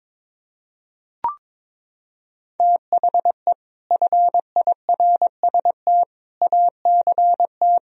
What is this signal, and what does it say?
Morse code, heard as a keyed tone, about 22 wpm, 700 Hz. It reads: THE FIRST ACT
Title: Sentences from Top 1000 Words - Review Part 1 - 22wpm